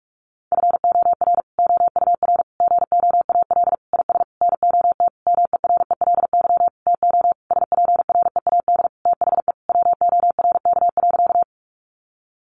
Morse code, heard as a keyed tone, about 44 wpm, 700 Hz. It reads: FOR OUR GOAL IS NOT MERELY TO SPREAD THE WORK.
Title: Sentences from Top 1400 Words 45wpm